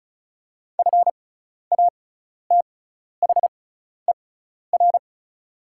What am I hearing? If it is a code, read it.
Morse code, heard as a keyed tone, about 34 wpm, spaced as for 12 wpm, 700 Hz. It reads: FATHER
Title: 6x Character Spacing - Top 300 Words - 35wpm